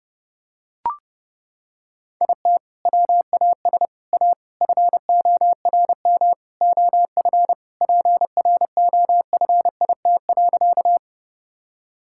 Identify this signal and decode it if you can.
Morse code, heard as a keyed tone, about 30 wpm, 700 Hz. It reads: IT WAS A FORM OF PROFIT.
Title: Sentences from Top 1000 Words - Review Part 5 - 30wpm